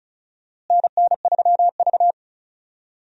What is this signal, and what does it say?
Morse code, heard as a keyed tone, about 34 wpm, 700 Hz. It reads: NN3V
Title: USA Call Signs - 4 Characters - Repeated 2x - 35wpm